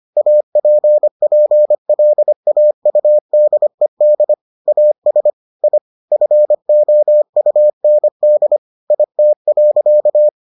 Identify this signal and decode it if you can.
Morse code, heard as a keyed tone, about 25 wpm, 600 Hz. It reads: APPLAUDED AS I FOUND IT.